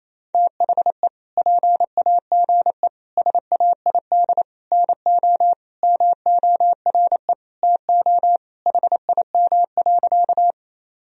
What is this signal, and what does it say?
Morse code, heard as a keyed tone, about 28 wpm, 700 Hz. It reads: THE PAGE SAID NO MORE TO HIM.
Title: Sentences from Top 1000 Words - Review Part 1 - 28wpm